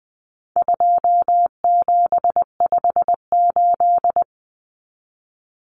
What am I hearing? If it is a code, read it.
Morse code, heard as a keyed tone, about 20 wpm, 700 Hz. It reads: 2758
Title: Numbers - 4-Digit 20wpm